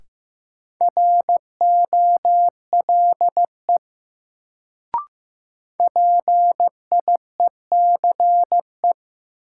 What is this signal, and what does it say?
Morse code, heard as a keyed tone, about 15 wpm, 700 Hz. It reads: ROLE PIECE
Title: Sets of 3 Words - New Word Drill - 15wpm